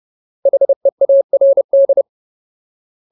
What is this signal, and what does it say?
Morse code, heard as a keyed tone, about 30 wpm, 550 Hz. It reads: HEARD
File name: Ultimate Word List / 30WPM Ultimate